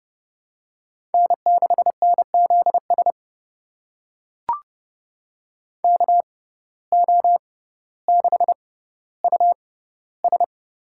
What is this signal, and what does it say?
Morse code, heard as a keyed tone, about 30 wpm, spaced as for 15 wpm, 700 Hz. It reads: N6NZS KO6US
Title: USA Call Signs - 5 Characters - 6x Character Spacing - 30wpm